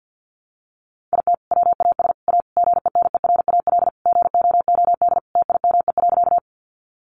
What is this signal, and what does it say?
Morse code, heard as a keyed tone, about 50 wpm, 700 Hz. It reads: IT WAS A GENERAL GOOD TIME.
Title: Sentences from Top 600 Words 50wpm